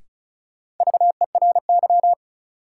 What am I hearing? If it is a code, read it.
Morse code, heard as a keyed tone, about 36 wpm, 700 Hz. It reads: VERY
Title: Sentences from Top 200 Words - New Word Drill - 35wpm